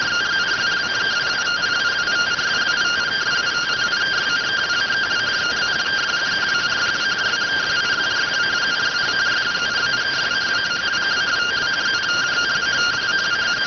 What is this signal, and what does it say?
Two-tone teleprinter signal, about 170 Hz shift.